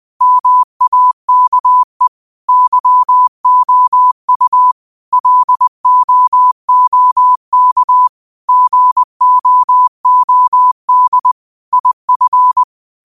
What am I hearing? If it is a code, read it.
Morse code, heard as a keyed tone, about 20 wpm, 1000 Hz. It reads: MAKE YOU LOOK GOOD IF
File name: 20wpm